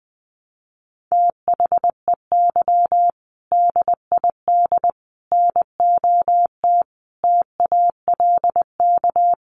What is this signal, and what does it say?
Morse code, heard as a keyed tone, about 20 wpm, 700 Hz. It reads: THEY DID NOT TALK
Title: Sentences from Top 1200 Words 20wpm